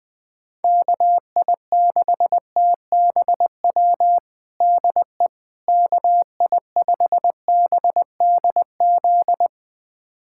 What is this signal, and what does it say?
Morse code, heard as a keyed tone, about 20 wpm, 700 Hz. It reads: KI6TBW DE KI5BDZ
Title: USA Callsign DE Callsign - 6 Characters - 20wpm